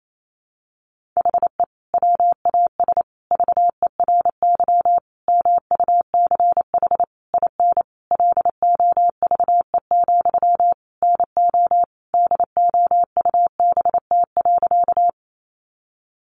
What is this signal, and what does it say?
Morse code, heard as a keyed tone, about 28 wpm, 700 Hz. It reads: HE WAS VERY MUCH IN LOVE, NO DOUBT.
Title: Sentences from Top 1500 Words 28wpm